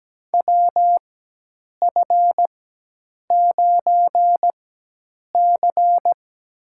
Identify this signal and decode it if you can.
Morse code, heard as a keyed tone, about 17 wpm, spaced as for 8 wpm, 700 Hz. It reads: WF9C
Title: USA Call Signs - 4 Characters - 4x Character Spacing - 17wpm